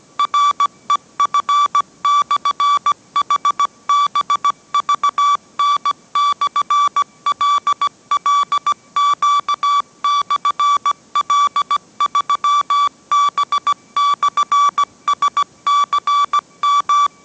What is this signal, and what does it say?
Morse code, heard as a keyed tone, about 17 wpm, 1150 Hz. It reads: REF/HBVN/LLQ/L3B/SCM